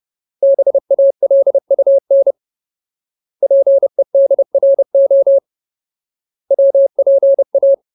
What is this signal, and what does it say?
Morse code, heard as a keyed tone, about 30 wpm, 550 Hz. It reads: BALUN PEDRO WPA
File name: Ultimate Word List / 30WPM Ultimate